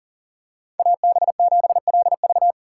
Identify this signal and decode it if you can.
Morse code, heard as a keyed tone, about 40 wpm, 700 Hz. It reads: AB7LV
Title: USA Call Signs - 5 Characters - 3x Character Spacing - 40wpm